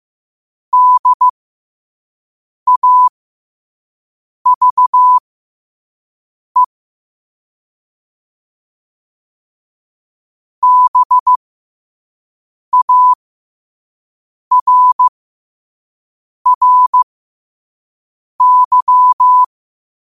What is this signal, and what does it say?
Morse code, heard as a keyed tone, about 15 wpm, spaced as for 5 wpm, 1000 Hz. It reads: DAVE BARRY
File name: qotdfiles / 5wpm